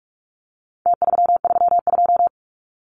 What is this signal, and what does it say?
Morse code, heard as a keyed tone, about 44 wpm, 700 Hz. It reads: T332
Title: Numbers - Contest Style - 4-Digit 45wpm